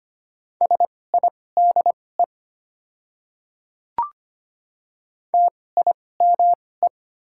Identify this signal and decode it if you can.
Morse code, heard as a keyed tone, about 25 wpm, spaced as for 18 wpm, 700 Hz. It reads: SIDE TIME
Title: 2x Character Spacing - Top 200 Words - 25wpm